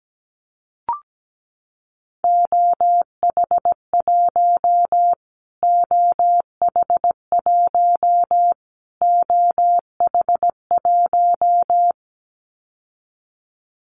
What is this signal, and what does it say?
Morse code, heard as a keyed tone, about 17 wpm, 700 Hz. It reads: OH1 OH1 OH1
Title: Callsign Prefixes for Top 25 DXCC - Repeated 3x - 17wpm